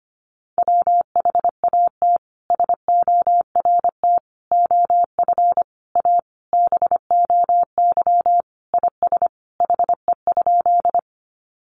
Morse code, heard as a keyed tone, about 25 wpm, 700 Hz. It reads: WHAT SORT OF A BOY IS HE?
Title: Sentences from Top 1300 Words 25wpm